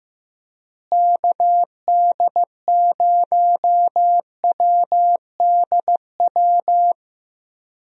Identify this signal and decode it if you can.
Morse code, heard as a keyed tone, about 15 wpm, 700 Hz. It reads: KD0WDW
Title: USA Call Signs - 6 Characters - Repeated 3x - 15wpm